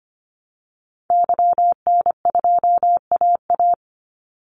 Morse code, heard as a keyed tone, about 25 wpm, 700 Hz. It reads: YN2AA
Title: Call Signs Repeated 3x - 25wpm